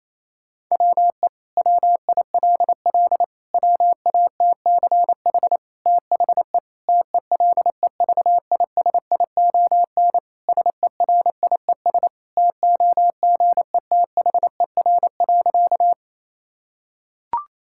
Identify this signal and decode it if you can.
Morse code, heard as a keyed tone, about 28 wpm, 700 Hz. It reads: WE WILL WATCH THE TELEVISION SERIES TOGETHER.
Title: Sentences from Top 1000 Words - Review Part 5 - 28wpm